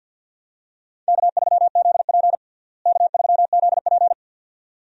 Morse code, heard as a keyed tone, about 52 wpm, 700 Hz. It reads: K3ZP K3ZP
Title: USA Call Signs - 4 Characters - Repeated 2x - 50wpm